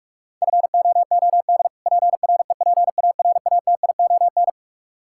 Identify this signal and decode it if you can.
Morse code, heard as a keyed tone, about 44 wpm, 700 Hz. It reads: FOOD PREPARATION
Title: Practice QSO - Exchange 3 - More Info - 45wpm